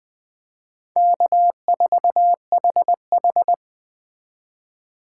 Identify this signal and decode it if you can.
Morse code, heard as a keyed tone, about 21 wpm, 700 Hz. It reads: K4HH
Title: Speed Racing - USA Call Signs 17wpm